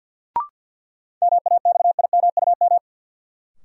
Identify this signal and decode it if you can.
Morse code, heard as a keyed tone, about 52 wpm, 700 Hz. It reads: MAXIMUM
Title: Sentences from Top 1300 Words - New Word Drill - 50wpm